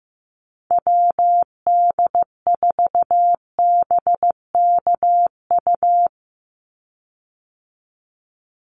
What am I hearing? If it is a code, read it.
Morse code, heard as a keyed tone, about 15 wpm, 700 Hz. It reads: WD4BKU
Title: USA Call Signs 15wpm